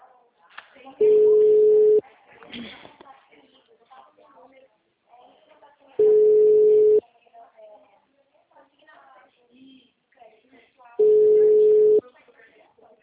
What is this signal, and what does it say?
Ringback tone (Europe, 425 Hz), with speech.